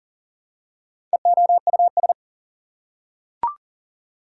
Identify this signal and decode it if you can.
Morse code, heard as a keyed tone, about 40 wpm, 700 Hz. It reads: EOUS
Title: Prefixes and Suffixes 40wpm